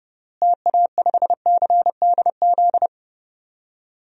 Morse code, heard as a keyed tone, about 30 wpm, 700 Hz. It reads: TA5CDZ
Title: DXCC-100 Call Signs - 6 Characters - Repeated 2x - 30wpm